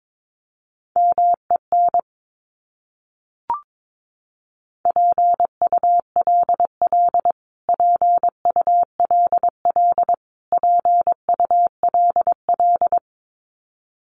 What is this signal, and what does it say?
Morse code, heard as a keyed tone, about 22 wpm, 700 Hz. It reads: MEN PULL PULL PULL
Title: Top 500 Words Repeated 22wpm